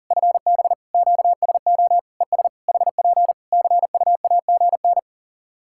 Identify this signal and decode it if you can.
Morse code, heard as a keyed tone, about 37 wpm, 700 Hz. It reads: FB QSO ES HP CUAGN
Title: Practice QSO - Exchange 4 - Ending - 40wpm